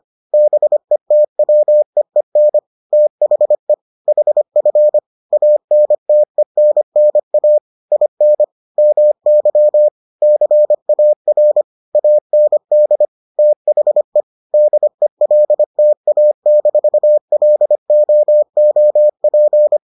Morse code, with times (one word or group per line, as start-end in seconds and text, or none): 0.3-2.6 BETWEEN
2.9-3.7 THE
4.1-5.0 HF
5.3-7.6 ANTENNA
7.9-8.4 IN
8.8-9.9 MY
10.2-11.6 CAR
11.9-13.1 AND
13.4-14.2 THE
14.5-19.8 DELTA-LOOP